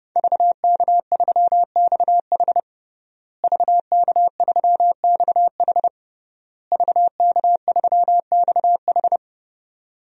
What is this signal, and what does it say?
Morse code, heard as a keyed tone, about 32 wpm, 700 Hz. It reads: VK3XH VK3XH VK3XH
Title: DXCC-100 Call Signs - 5 Characters - Repeated 3x - 30wpm